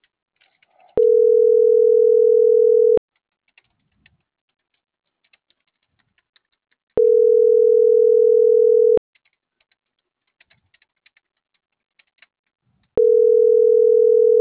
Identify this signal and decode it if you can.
Ringback tone (North America), 440 + 480 Hz.
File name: CENTRAL MAERSK / out andres.rdzmsk